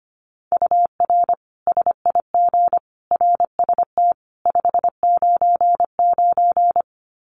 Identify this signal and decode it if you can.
Morse code, heard as a keyed tone, about 25 wpm, 700 Hz. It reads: UR SIG RST 599
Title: Common QSO Phrases 25wpm